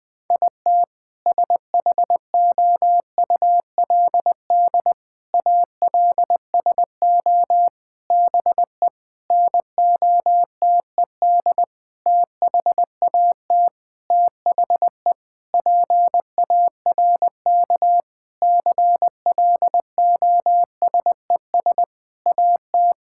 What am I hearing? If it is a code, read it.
Morse code, heard as a keyed tone, about 20 wpm, 700 Hz. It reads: IT SHOULD ALSO BE NOTED THAT THE PARK CLOSES AT
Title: Sentences with 7-Word Phrases - Spoken Beginning - 20wpm